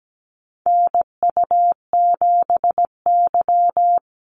Morse code, heard as a keyed tone, about 17 wpm, 700 Hz. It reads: NU7Y
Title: Call Signs Repeated 2x - 17wpm